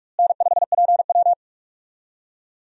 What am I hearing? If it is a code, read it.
Morse code, heard as a keyed tone, about 46 wpm, 700 Hz. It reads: N5PW
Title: USA Call Signs - 4 Characters - Repeated 2x - 45wpm